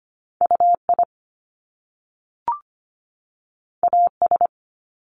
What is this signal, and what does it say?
Morse code, heard as a keyed tone, about 25 wpm, 700 Hz. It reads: UI AS
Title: Top Two Letter Combinations in Words 25wpm